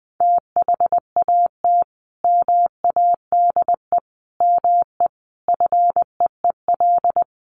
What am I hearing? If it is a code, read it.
Morse code, heard as a keyed tone, about 19 wpm, 700 Hz. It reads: THAT MADE ME FEEL
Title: Sentences from Top 1000 Words 20wpm